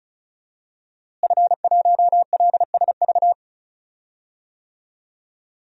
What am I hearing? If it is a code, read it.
Morse code, heard as a keyed tone, about 36 wpm, 700 Hz. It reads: F1LSV
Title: DXCC-100 Call Signs - 5 Characters - Repeated 2x - 35wpm